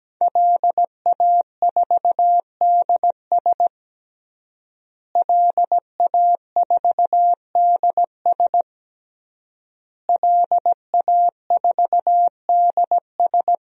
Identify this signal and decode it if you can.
Morse code, heard as a keyed tone, about 17 wpm, 700 Hz. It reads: LA4DS LA4DS LA4DS
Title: DXCC-100 Call Signs - 5 Characters - Repeated 3x - 17wpm